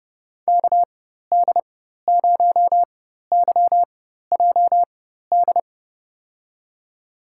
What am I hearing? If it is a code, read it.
Morse code, heard as a keyed tone, about 31 wpm, spaced as for 14 wpm, 700 Hz. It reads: KD0YJD
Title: USA Call Signs - 6 Characters - 4x Character Spacing - 30wpm